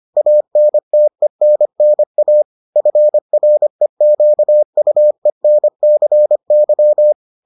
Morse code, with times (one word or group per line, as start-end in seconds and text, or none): 0.2-2.4 ANTENNA
2.8-7.1 FREQUENCY